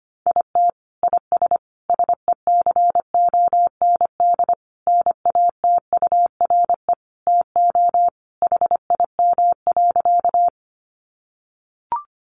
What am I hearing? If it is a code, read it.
Morse code, heard as a keyed tone, about 25 wpm, 700 Hz. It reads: IT IS SECOND NATURE TO HIM.
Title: Sentences from Top 600 Words 25wpm